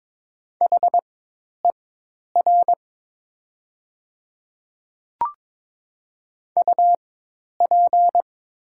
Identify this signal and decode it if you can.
Morse code, heard as a keyed tone, about 22 wpm, spaced as for 10 wpm, 690 Hz. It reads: HER UP
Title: 4x Character Spacing - Top 100 Words - 22wpm